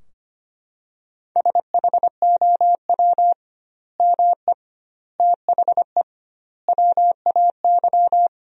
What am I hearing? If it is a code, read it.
Morse code, heard as a keyed tone, about 25 wpm, 700 Hz. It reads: SHOW ME THE WAY
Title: 2x Word-Spacing - Sentences from Top 200 Words - 25wpm